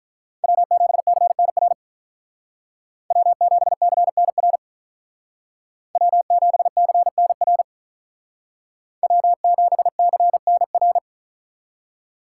Morse code, heard as a keyed tone, about 45 wpm, 700 Hz. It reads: W7CNR W7CNR W7CNR W7CNR